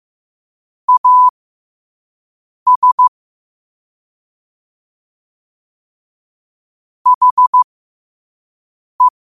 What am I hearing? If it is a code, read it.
Morse code, heard as a keyed tone, about 15 wpm, spaced as for 5 wpm, 1000 Hz. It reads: AS HE
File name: qotdfiles / 5wpm